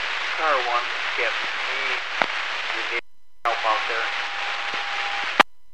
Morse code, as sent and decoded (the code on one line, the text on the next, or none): none